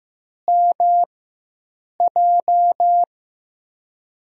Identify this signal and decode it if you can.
Morse code, heard as a keyed tone, about 15 wpm, spaced as for 7 wpm, 700 Hz. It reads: MJ
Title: USA Call Signs - 4x Character Spacing - 15wpm